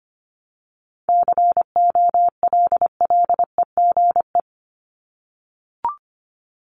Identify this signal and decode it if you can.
Morse code, heard as a keyed tone, about 25 wpm, 700 Hz. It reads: COLLEGE
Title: Top 1001-1250 Words 25wpm